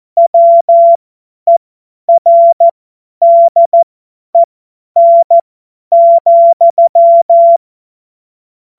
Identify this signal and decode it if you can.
Morse code, heard as a keyed tone, about 14 wpm, spaced as for 10 wpm, 680 Hz. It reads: WERDEN,
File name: Morsekurs / Witze / Witz 06 Februar 2024 50 F70